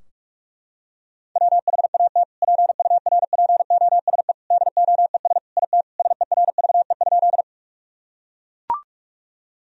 Morse code, heard as a keyed tone, about 41 wpm, 700 Hz. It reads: WHAT PURPOSE DOES IT SERVE?
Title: Sentences from Top 1000 Words - Review Part 2 - 45wpm